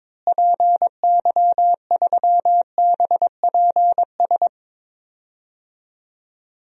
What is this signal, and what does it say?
Morse code, heard as a keyed tone, about 22 wpm, 700 Hz. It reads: PY3BPS
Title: DXCC-100 Call Signs - 6 Characters - Repeated 2x - 22wpm